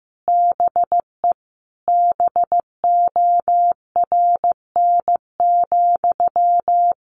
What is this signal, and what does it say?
Morse code, heard as a keyed tone, about 15 wpm, 700 Hz. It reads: BE BORN,
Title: A Sound Of Thunder 15wpm